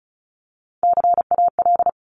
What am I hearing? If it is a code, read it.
Morse code, heard as a keyed tone, about 36 wpm, 700 Hz. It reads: CAL
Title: Top Three Letter Combinations in Words 35wpm